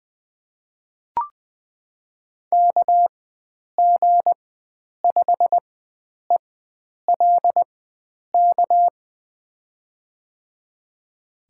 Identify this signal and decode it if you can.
Morse code, heard as a keyed tone, about 20 wpm, spaced as for 9 wpm, 700 Hz. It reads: KG5ELK
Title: USA Call Signs - 6 Characters - 4x Character Spacing - 20wpm